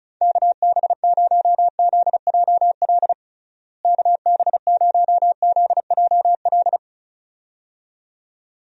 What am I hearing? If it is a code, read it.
Morse code, heard as a keyed tone, about 36 wpm, 700 Hz. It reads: KB0ZJL KB0ZJL
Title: USA Call Signs - 6 Characters - Repeated 2x - 35wpm